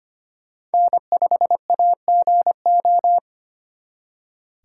Morse code, heard as a keyed tone, about 29 wpm, 700 Hz. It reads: N5AGO
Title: Speed Racing - USA 1x3 Call Signs 25wpm